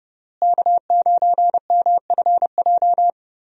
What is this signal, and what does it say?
Morse code, heard as a keyed tone, about 32 wpm, 700 Hz. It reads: K9MFJ
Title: USA Call Signs - 5 Characters - Repeated 3x - 30wpm